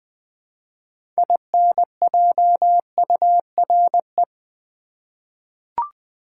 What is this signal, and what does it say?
Morse code, heard as a keyed tone, about 20 wpm, 700 Hz. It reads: INJURE
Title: 6 Letter Words - First 3 Letters Given - 20wpm